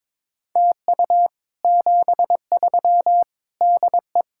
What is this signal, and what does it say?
Morse code, heard as a keyed tone, about 22 wpm, 700 Hz. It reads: TU 73 DE
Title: Speed-Racing Common QSO Phrases - 22wpm